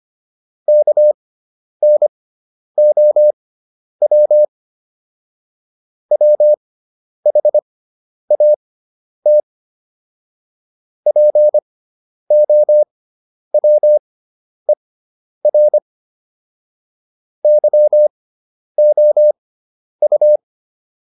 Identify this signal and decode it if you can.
Morse code, heard as a keyed tone, about 25 wpm, spaced as for 10 wpm, 600 Hz. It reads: KNOW WHAT POWER YOU
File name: QSO409 10